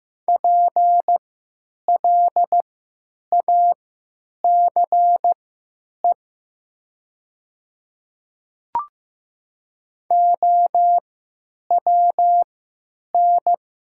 Morse code, heard as a keyed tone, about 15 wpm, spaced as for 9 wpm, 700 Hz. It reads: PLACE OWN
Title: 3x Character Spacing - Top 300 Words - 15wpm